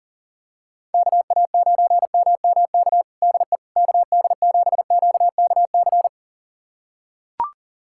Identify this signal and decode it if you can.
Morse code, heard as a keyed tone, about 40 wpm, 700 Hz. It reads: KA9MMK DE KD7QKC
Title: USA Callsign DE Callsign 40wpm